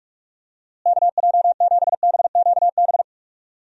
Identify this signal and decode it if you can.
Morse code, heard as a keyed tone, about 46 wpm, 700 Hz. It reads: KJ7BQB
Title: USA Call Signs - 6 Characters - Repeated 3x - 45wpm